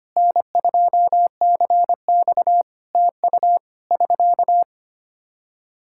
Morse code, heard as a keyed tone, about 25 wpm, 700 Hz. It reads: N2CX TU <SK>
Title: Practice QSO - Exchange 4 - Ending - 25wpm